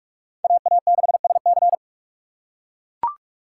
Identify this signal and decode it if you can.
Morse code, heard as a keyed tone, about 44 wpm, 700 Hz. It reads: AA6SC